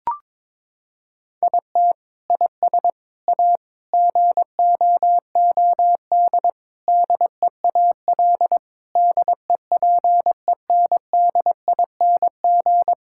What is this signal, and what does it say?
Morse code, heard as a keyed tone, about 21 wpm, 700 Hz. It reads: IT IS A GOOD DEAL DEPENDING
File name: Sentences with 7-Word Phrases - 22wpm